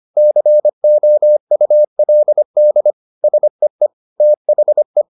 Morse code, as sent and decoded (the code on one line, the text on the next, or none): -.-. --- ..- .-.. -.. / ... . / - .... .
COULD SEE THE